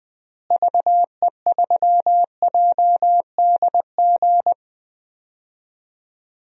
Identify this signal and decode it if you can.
Morse code, heard as a keyed tone, about 21 wpm, 700 Hz. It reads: VE3JDG
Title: DXCC-100 Call Signs - 6 Characters - Repeated 3x - 20wpm